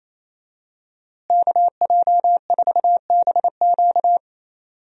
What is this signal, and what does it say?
Morse code, heard as a keyed tone, about 28 wpm, 700 Hz. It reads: KJ4BQ
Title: USA Call Signs - 5 Characters - Repeated 2x - 28wpm